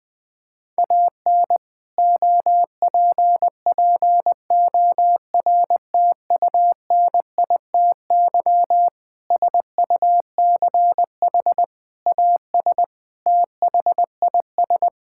Morse code, heard as a keyed tone, about 20 wpm, 700 Hz. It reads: AN OPPORTUNITY SUCH AS THIS
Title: Sentences from Top 1600 Words 20wpm